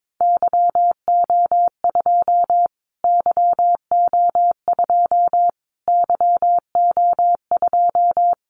Morse code, heard as a keyed tone, about 22 wpm, 700 Hz. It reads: YO2 YO2 YO2
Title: Callsign Prefixes for Top 25 DXCC - Repeated 3x - 22wpm